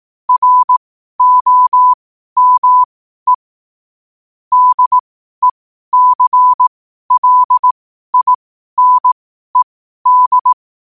Morse code, heard as a keyed tone, about 18 wpm, spaced as for 12 wpm, 1000 Hz. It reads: ROME DECLINED
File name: qotdfiles / 12wpm